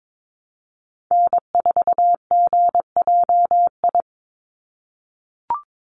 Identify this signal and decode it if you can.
Morse code, heard as a keyed tone, about 22 wpm, 700 Hz. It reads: N4GJI
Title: USA Call Signs 22wpm